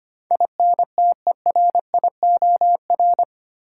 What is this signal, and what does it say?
Morse code, heard as a keyed tone, about 25 wpm, 700 Hz. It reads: INTERIOR
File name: 8 Letter Words - First 3 Letters Given - 25wpm